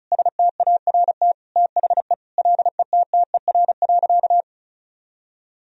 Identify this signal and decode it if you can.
Morse code, heard as a keyed tone, about 35 wpm, 700 Hz. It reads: START THE LETTER.
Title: Sentences from Top 1000 Words - Review Part 1 - 35wpm